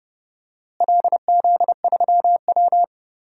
Morse code, heard as a keyed tone, about 32 wpm, 700 Hz. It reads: LZ3W
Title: DXCC-100 Call Signs - 4 Characters - Repeated 2x - 30wpm